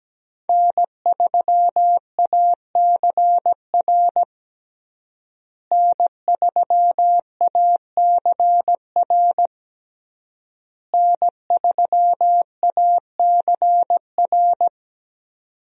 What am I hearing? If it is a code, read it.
Morse code, heard as a keyed tone, about 17 wpm, 700 Hz. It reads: N3ACR N3ACR N3ACR
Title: USA Call Signs - 5 Characters - Repeated 3x - 17wpm